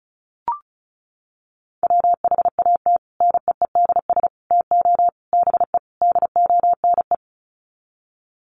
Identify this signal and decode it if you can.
Morse code, heard as a keyed tone, about 34 wpm, 700 Hz. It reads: WHAT NEEDS TO BE DONE
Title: Sets of 5 Words Encore 35wpm